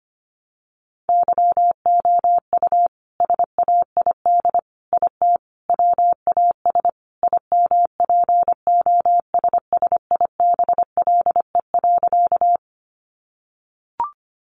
Morse code, heard as a keyed tone, about 25 wpm, 700 Hz. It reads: YOU SAID IT WAS IMPOSSIBLE.
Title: Sentences from Top 1100 Words 25wpm